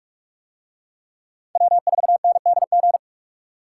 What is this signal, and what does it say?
Morse code, heard as a keyed tone, about 42 wpm, 700 Hz. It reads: W4NDG